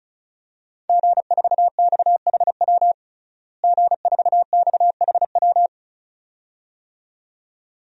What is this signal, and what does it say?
Morse code, heard as a keyed tone, about 36 wpm, 700 Hz. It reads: G4XHW G4XHW
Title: DXCC-100 Call Signs - 5 Characters - Repeated 2x - 35wpm